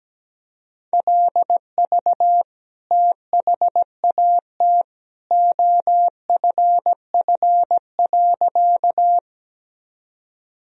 Morse code, heard as a keyed tone, about 17 wpm, 700 Hz. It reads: LV THAT OFF.
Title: Sentences from Top 400 Words and Abbreviations 17wpm